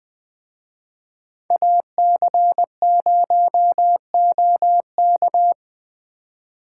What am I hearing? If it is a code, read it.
Morse code, heard as a keyed tone, about 20 wpm, 700 Hz. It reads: AC0OK